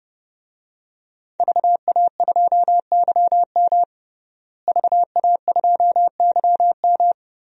Morse code, heard as a keyed tone, about 30 wpm, 700 Hz. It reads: VA2YM VA2YM
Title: DXCC-100 Call Signs - 5 Characters - Repeated 3x - 30wpm